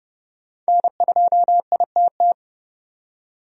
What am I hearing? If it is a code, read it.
Morse code, heard as a keyed tone, about 29 wpm, 700 Hz. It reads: N2ITT
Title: USA Call Signs - 5 Characters - 30wpm